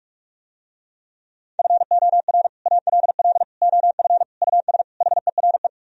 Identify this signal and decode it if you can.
Morse code, heard as a keyed tone, about 48 wpm, 700 Hz. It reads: FOR ALL OF US HERE